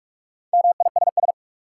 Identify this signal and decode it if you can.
Morse code, heard as a keyed tone, about 46 wpm, 700 Hz. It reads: MISS
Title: Sentences from Top 500 Words - New Word Drill - 45wpm